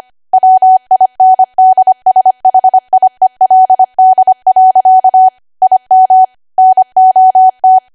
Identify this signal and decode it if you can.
Morse code, heard as a keyed tone, about 25 wpm, 750 Hz. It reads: WINDSHIELD. IM NOT
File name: Archive / 25 WPM / 25WPM